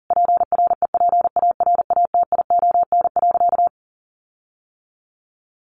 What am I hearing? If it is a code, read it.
Morse code, heard as a keyed tone, about 40 wpm, 700 Hz. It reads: PREPARATION.
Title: Sentences from Top 1400 Words 40wpm